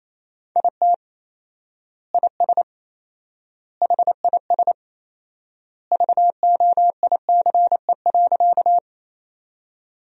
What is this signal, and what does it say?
Morse code, heard as a keyed tone, about 28 wpm, 700 Hz. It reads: IT IS HIS VOICE.
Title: 4x Word-Spacing - Sentences from Top 500 Words - 28wpm